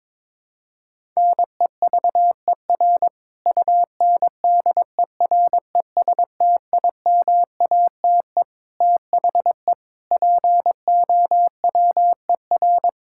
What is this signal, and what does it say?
Morse code, heard as a keyed tone, about 21 wpm, 700 Hz. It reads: NEVER UNDERESTIMATE THE POWER
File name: practice / 22wpm / Sentences with 4-Word Phrases - 22wpm